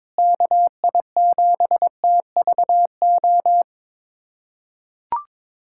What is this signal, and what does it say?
Morse code, heard as a keyed tone, about 22 wpm, 700 Hz. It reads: KI7TVO
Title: USA Callsign DE Callsign - 6 Characters - 22wpm